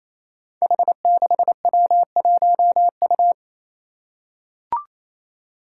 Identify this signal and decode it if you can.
Morse code, heard as a keyed tone, about 28 wpm, 700 Hz. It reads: H6W1U
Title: Five Challenging Letter-Number Combinations 28wpm